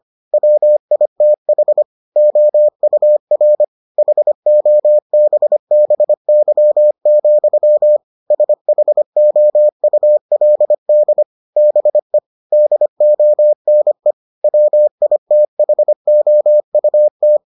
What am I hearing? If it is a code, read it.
Morse code, heard as a keyed tone, about 25 wpm, 600 Hz. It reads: WITH OUR HOBBY, SHOULD BE DONE WITHOUT